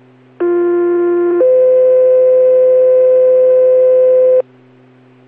Two-tone page, heard as a one-second tone, then a three-second tone, about 350 then 510 Hz. It is heard from 0.4 to 4.4 s.